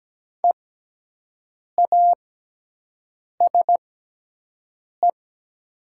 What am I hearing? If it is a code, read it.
Morse code, heard as a keyed tone, about 17 wpm, spaced as for 6 wpm, 700 Hz. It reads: EASE
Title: 6x Character Spacing - Top 300 Words - 17wpm